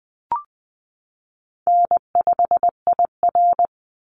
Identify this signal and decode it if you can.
Morse code, heard as a keyed tone, about 20 wpm, 700 Hz. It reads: N5IR
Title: Easy Call Signs - 20wpm